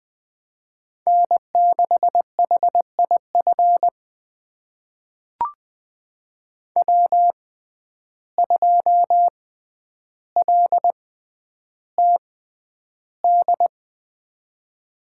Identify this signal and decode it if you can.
Morse code, heard as a keyed tone, about 20 wpm, spaced as for 10 wpm, 700 Hz. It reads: N6HIF W2LTD
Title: USA Call Signs - 5 Characters - 6x Character Spacing - 20wpm